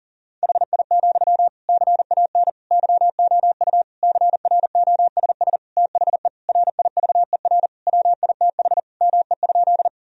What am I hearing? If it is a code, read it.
Morse code, heard as a keyed tone, about 40 wpm, 700 Hz. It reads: HI, CAN YOU CROSS THE RIVER WITH ME?
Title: Sentences from Top 1000 Words - Review Part 3 - 40wpm